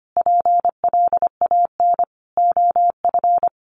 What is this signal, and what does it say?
Morse code, heard as a keyed tone, about 25 wpm, 700 Hz. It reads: PLAN OF
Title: Sentences from Top 1200 Words 25wpm